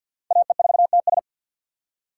Morse code, heard as a keyed tone, about 53 wpm, 700 Hz. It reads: AE4TS
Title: USA Call Signs - 5 Characters - 4x Character Spacing - 50wpm